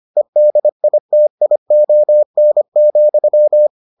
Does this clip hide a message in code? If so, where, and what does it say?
Yes: Morse code from 0.2 to 3.7 s, heard as a keyed tone, about 25 wpm, 600 Hz. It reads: EDITION,